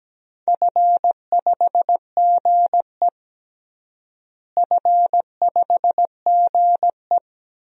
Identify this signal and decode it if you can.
Morse code, heard as a keyed tone, about 17 wpm, 700 Hz. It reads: F5GE F5GE
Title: DXCC-100 Call Signs - 4 Characters - Repeated 2x - 17wpm